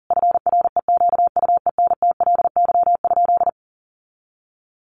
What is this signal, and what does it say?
Morse code, heard as a keyed tone, about 40 wpm, 700 Hz. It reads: FREQUENTLY?
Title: Sentences from Top 1300 Words 40wpm